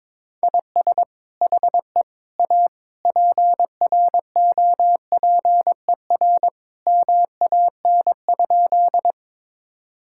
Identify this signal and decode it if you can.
Morse code, heard as a keyed tone, about 22 wpm, 700 Hz. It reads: IS HE A PROPER MAN?
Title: Sentences from Top 1000 Words - Review Part 5 - 22wpm